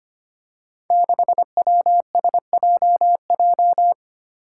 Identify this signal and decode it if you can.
Morse code, heard as a keyed tone, about 25 wpm, 700 Hz. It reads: 6WSJJ
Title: Five Challenging Letter-Number Combinations 25wpm